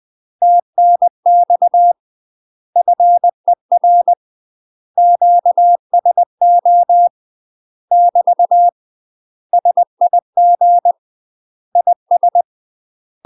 Morse code, heard as a keyed tone, about 20 wpm, 700 Hz. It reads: TNX FER QSO <BT> SIG IS
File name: UK MASTER QSOs TRAINING / 20WPM / UK MASTER QSOs 0013